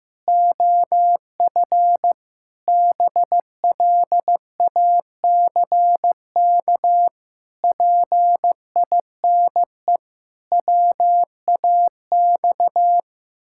Morse code, heard as a keyed tone, about 15 wpm, 700 Hz. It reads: OF BLACK PINE WAX
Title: Pangram Challenge v2 - 15wpm